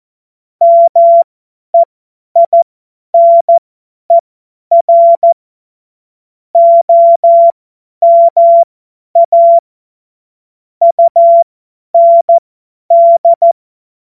Morse code, heard as a keyed tone, about 14 wpm, spaced as for 10 wpm, 680 Hz. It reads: MEINER OMA UND
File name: Morsekurs / Witze / Witz 05. Oktober 2021, 50 F70